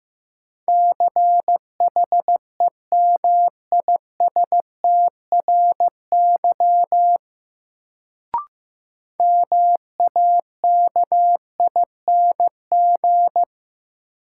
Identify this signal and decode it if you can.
Morse code, heard as a keyed tone, about 15 wpm, 700 Hz. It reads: CHEMISTRY MAKING
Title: Sentences from Top 1000 Words - Review Part 6 - New Word Drill - 15wpm